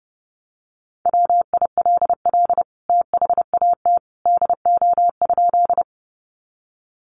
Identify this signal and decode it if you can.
Morse code, heard as a keyed tone, about 29 wpm, 700 Hz. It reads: WILL THAT DO?
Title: Sentences from Top 100 Words 30wpm